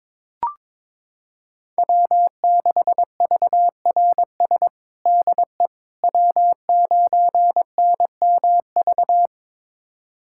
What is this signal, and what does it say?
Morse code, heard as a keyed tone, about 22 wpm, 700 Hz. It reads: W6VRS DE W9NMV
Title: USA Callsign DE Callsign - 5 Characters - 22wpm